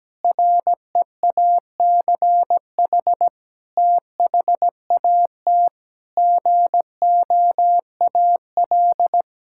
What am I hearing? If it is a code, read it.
Morse code, heard as a keyed tone, about 17 wpm, 700 Hz. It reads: REACH THAT GOAL
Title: Sentences from Top 1000 Words - Review Part 5 - 17wpm